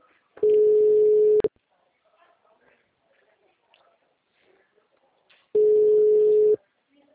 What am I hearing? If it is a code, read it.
Ringback tone (Europe, 425 Hz).
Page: Nova Odessa - SP